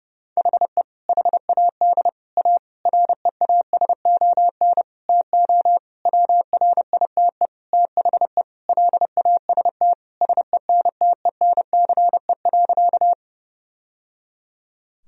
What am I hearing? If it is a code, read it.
Morse code, heard as a keyed tone, about 32 wpm, 700 Hz. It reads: HE HAD A REASON TO WRITE THE LAST SENTENCE.
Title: Sentences from Top 1000 Words - Review Part 1 - 30wpm